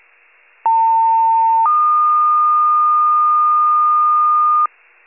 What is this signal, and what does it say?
Two-tone page, heard as a one-second tone, then a three-second tone, about 900 then 1220 Hz. It is heard from 0.6 to 4.7 s.
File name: wcalert